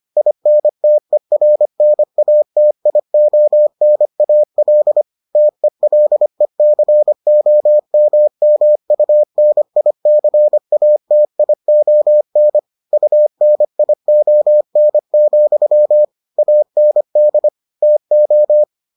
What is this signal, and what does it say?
Morse code, heard as a keyed tone, about 25 wpm, 600 Hz. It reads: INTERNATIONAL TELECOMMUNICATION UNION, AND TO